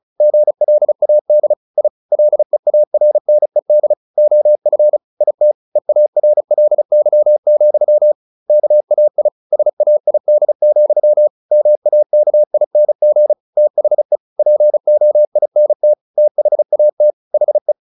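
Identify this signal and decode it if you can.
Morse code, heard as a keyed tone, about 34 wpm, 600 Hz. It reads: GLAD I LEARNED OF IT EARLY, KAI SAID, MAKING THE POINT THAT HE
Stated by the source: Morse practice at 35 wpm from CW med Gnister, sending Ken Follett's novel Never in English.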